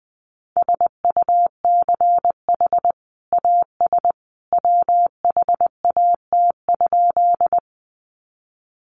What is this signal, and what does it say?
Morse code, heard as a keyed tone, about 20 wpm, 700 Hz. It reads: SUCH AS WHAT?
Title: Sentences from Top 200 Words 20wpm